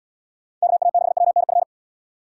Morse code, heard as a keyed tone, about 69 wpm, 700 Hz. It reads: KA6WNF